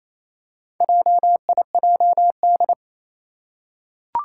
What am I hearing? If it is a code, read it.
Morse code, heard as a keyed tone, about 27 wpm, 700 Hz. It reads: JIJD
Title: Four Challenging Letter-Number Combinations 28wpm